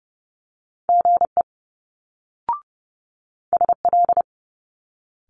Morse code, heard as a keyed tone, about 31 wpm, 700 Hz. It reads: GE SL
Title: Multi-letter Phonemes Quick Fire 30wpm